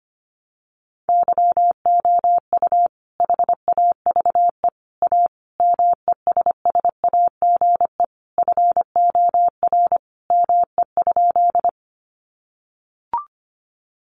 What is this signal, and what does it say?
Morse code, heard as a keyed tone, about 25 wpm, 700 Hz. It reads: YOU HAVE A MESSAGE FOR ME?
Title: Sentences from Top 1100 Words 25wpm